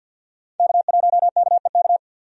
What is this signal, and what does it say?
Morse code, heard as a keyed tone, about 50 wpm, 700 Hz. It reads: K1KEK
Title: USA Call Signs - 5 Characters - 50wpm